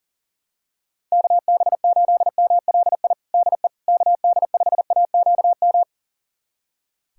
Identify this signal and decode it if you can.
Morse code, heard as a keyed tone, about 40 wpm, 700 Hz. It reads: KB8MLI DE KD5AQM
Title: USA Callsign DE Callsign 40wpm